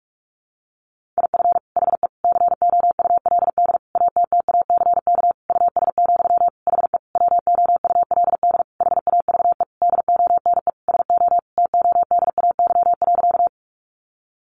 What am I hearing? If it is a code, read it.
Morse code, heard as a keyed tone, about 44 wpm, 700 Hz. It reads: IF HE COULD ATTACK US, HE WOULD HAVE DONE SO TODAY.
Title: Sentences from Top 1500 Words 45wpm